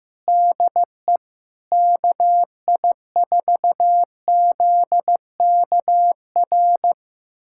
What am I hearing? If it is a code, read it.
Morse code, heard as a keyed tone, about 15 wpm, 700 Hz. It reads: DE KI4ZKR
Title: USA Callsign DE Callsign - 6 Characters - 15wpm